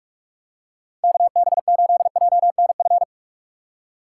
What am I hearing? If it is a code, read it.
Morse code, heard as a keyed tone, about 46 wpm, 700 Hz. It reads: KB8JNF
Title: USA Call Signs - 2x Character Spacing - 45wpm